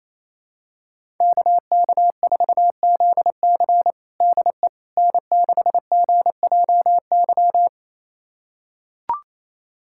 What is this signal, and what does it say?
Morse code, heard as a keyed tone, about 28 wpm, 700 Hz. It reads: KK4ZC DE N6GJY
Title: USA Callsign DE Callsign - 5 Characters - 28wpm